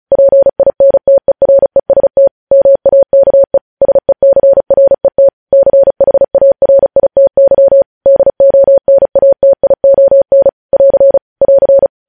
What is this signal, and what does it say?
Morse code, heard as a keyed tone, about 36 wpm, 550 Hz. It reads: PINTEREST MAKE SECRET CHARITY DONATION <AR> <AR>
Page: news headlines in morse code at 35 wpm